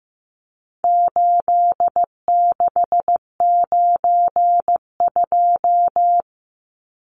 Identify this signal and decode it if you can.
Morse code, heard as a keyed tone, about 15 wpm, 700 Hz. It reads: 8692
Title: Numbers - 4-Digit 15wpm